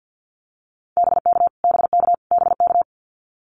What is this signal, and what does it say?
Morse code, heard as a keyed tone, about 49 wpm, 700 Hz. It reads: BK BK BK
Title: Abbreviations Repeated 3x - 50wpm